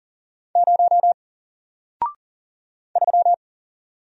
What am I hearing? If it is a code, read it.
Morse code, heard as a keyed tone, about 40 wpm, 700 Hz. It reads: 0 3
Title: Single Letter-Number 40wpm